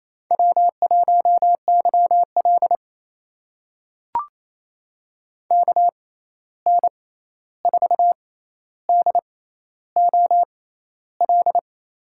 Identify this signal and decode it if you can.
Morse code, heard as a keyed tone, about 27 wpm, spaced as for 13 wpm, 700 Hz. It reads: W1YL KN4DOL
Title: USA Call Signs - 6x Character Spacing - 28wpm